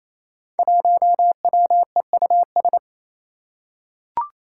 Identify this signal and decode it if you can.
Morse code, heard as a keyed tone, about 28 wpm, 700 Hz. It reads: 1WEUS
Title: Five Challenging Letter-Number Combinations 28wpm